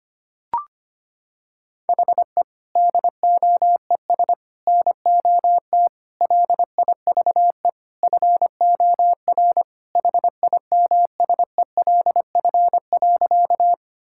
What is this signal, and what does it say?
Morse code, heard as a keyed tone, about 25 wpm, 700 Hz. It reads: HE DOES NOT LIVE FOR HIMSELF.
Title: Sentences from Top 1600 Words 25wpm